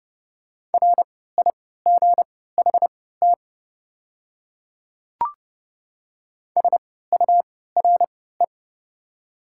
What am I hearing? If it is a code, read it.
Morse code, heard as a keyed tone, about 30 wpm, spaced as for 17 wpm, 700 Hz. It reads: RIGHT SURE
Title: 3x Character Spacing - Top 300 Words - 30wpm